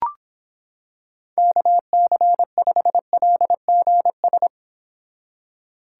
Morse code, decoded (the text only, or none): KC5LGS